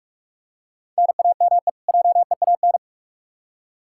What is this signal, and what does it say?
Morse code, heard as a keyed tone, about 46 wpm, 700 Hz. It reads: NAME JEAN
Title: Practice QSO - Exchange 1 - Beginning - 45wpm